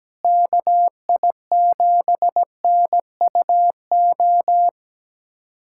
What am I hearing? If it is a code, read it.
Morse code, heard as a keyed tone, about 17 wpm, 700 Hz. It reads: KI7NUO
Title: USA Callsign DE Callsign - 6 Characters - 17wpm